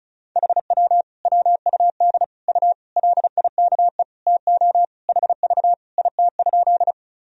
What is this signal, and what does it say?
Morse code, heard as a keyed tone, about 36 wpm, 700 Hz. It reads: HW WUD U LIKE TO HV IT?
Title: Sentences from Top 100 Words and Abbreviations 35wpm